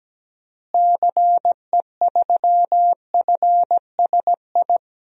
Morse code, heard as a keyed tone, about 17 wpm, 700 Hz. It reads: CE3FSI
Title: DXCC-100 Call Signs - 6 Characters - Repeated 2x - 17wpm